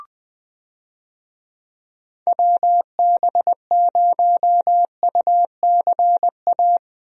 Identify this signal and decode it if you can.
Morse code, heard as a keyed tone, about 20 wpm, 700 Hz. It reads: WB0UCA
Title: USA Call Signs - 6 Characters - Repeated 2x - 20wpm